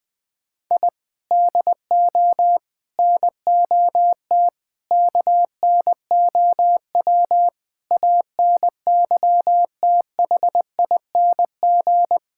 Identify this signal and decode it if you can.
Morse code, heard as a keyed tone, about 20 wpm, 700 Hz. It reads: I DO NOT KNOW ANYTHING
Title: Sentences with 5-Word Phrases - 20wpm